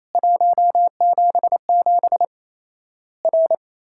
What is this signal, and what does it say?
Morse code, heard as a keyed tone, about 28 wpm, 700 Hz. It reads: 177 R
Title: Worked All Europe DX Contest - QTC Exchange - 28wpm